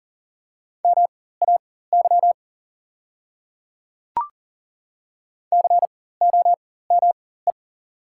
Morse code, heard as a keyed tone, about 40 wpm, spaced as for 19 wpm, 690 Hz. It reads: MAY COME
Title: 4x Character Spacing - Top 100 Words - 40wpm